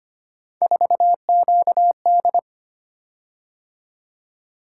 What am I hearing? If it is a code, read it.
Morse code, heard as a keyed tone, about 25 wpm, 700 Hz. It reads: 4QD
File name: Three Letter-Number Combinations - 25wpm